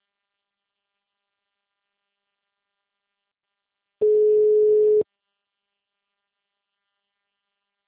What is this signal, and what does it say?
Ringback tone (Europe, 425 Hz).